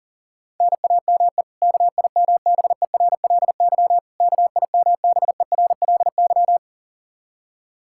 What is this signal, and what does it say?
Morse code, heard as a keyed tone, about 37 wpm, 700 Hz. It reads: NAME KIMBERLY KIMBERLY
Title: Practice QSO - Exchange 1 - Beginning - 40wpm